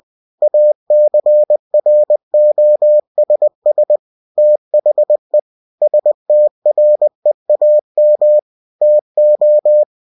Morse code, heard as a keyed tone, about 20 wpm, 600 Hz. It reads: ACROSS THE STREAM TO
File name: cw-books / winnie-the-pooh / winnie-20wpm-page-0057